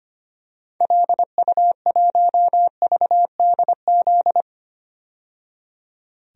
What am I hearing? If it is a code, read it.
Morse code, heard as a keyed tone, about 25 wpm, 690 Hz. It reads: LU1VDZ
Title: DXCC-100 Call Signs - 6 Characters - 25wpm